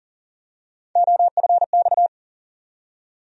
Morse code, heard as a keyed tone, about 40 wpm, 700 Hz. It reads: OFX
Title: USA Call Sign Postfixes - 3 Letters 40wpm